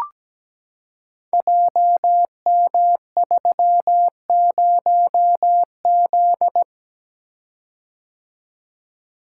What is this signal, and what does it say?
Morse code, heard as a keyed tone, about 17 wpm, 700 Hz. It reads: JM30Z
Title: Code Groups - 5 Letter:Numbers - 17wpm